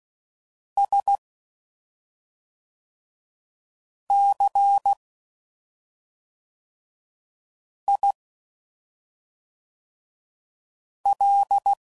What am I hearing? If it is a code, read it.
Morse code, heard as a keyed tone, about 16 wpm, 800 Hz. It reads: S C I L
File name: Suffixes / Special ING's / SlowToFaster ING's